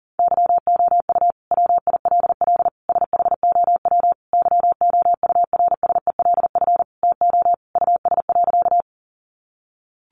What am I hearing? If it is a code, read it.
Morse code, heard as a keyed tone, about 40 wpm, 700 Hz. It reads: YOU WILL SHOW YOURSELF TO US.
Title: Sentences from Top 600 Words 40wpm